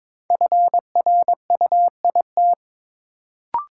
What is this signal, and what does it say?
Morse code, heard as a keyed tone, about 22 wpm, 700 Hz. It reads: FRUIT
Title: Sentences from Top 1300 Words - New Word Drill - 22wpm